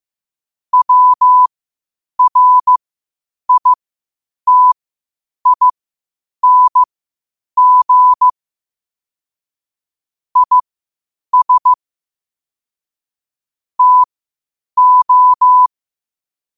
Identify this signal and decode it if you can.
Morse code, heard as a keyed tone, about 15 wpm, spaced as for 8 wpm, 1000 Hz. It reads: WRITING IS TO
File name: qotdfiles / 8wpm